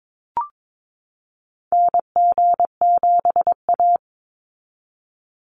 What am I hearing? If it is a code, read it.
Morse code, heard as a keyed tone, about 22 wpm, 700 Hz. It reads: NG7A
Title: Call Signs Repeated 3x - 22wpm